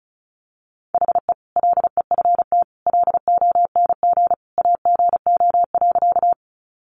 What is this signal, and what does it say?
Morse code, heard as a keyed tone, about 34 wpm, 700 Hz. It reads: HE LEFT LONG AGO.
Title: Sentences from Top 500 Words 35wpm